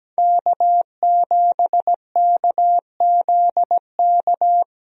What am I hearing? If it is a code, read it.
Morse code, heard as a keyed tone, about 17 wpm, 700 Hz. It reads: K7KZK
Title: USA Call Signs - 5 Characters - Repeated 2x - 17wpm